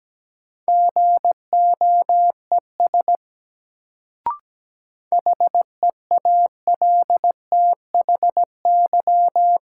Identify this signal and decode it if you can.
Morse code, heard as a keyed tone, about 17 wpm, 700 Hz. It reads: GOES HEALTHY
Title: Sentences from Top 1000 Words - Review Part 5 - New Word Drill - 17wpm